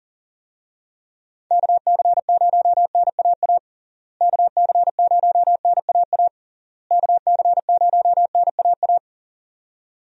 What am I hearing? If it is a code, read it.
Morse code, heard as a keyed tone, about 40 wpm, 700 Hz. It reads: KC0NAA KC0NAA KC0NAA
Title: USA Call Signs - 6 Characters - Repeated 3x - 40wpm